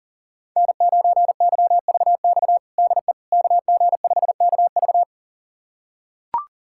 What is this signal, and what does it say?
Morse code, heard as a keyed tone, about 40 wpm, 700 Hz. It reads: N9YVX DE KG5KV
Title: USA Callsign DE Callsign - 5 Characters - 40wpm